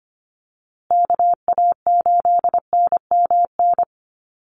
Morse code, heard as a keyed tone, about 25 wpm, 700 Hz. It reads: KA8NMN
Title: USA Call Signs Repeated 3x 25wpm